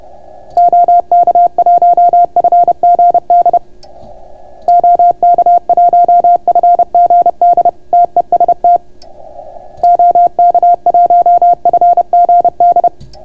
Morse code, heard as a keyed tone, about 30 wpm, 700 Hz. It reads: OK1FGD OK1FGD TEST OK1FGD